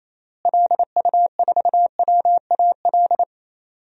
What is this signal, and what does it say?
Morse code, heard as a keyed tone, about 28 wpm, 700 Hz. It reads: LU4WAL
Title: DXCC-100 Call Signs - 6 Characters - Repeated 3x - 28wpm